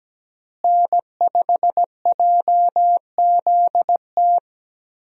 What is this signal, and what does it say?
Morse code, heard as a keyed tone, about 17 wpm, 700 Hz. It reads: N5JZT
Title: USA Call Signs - 5 Characters - Repeated 2x - 17wpm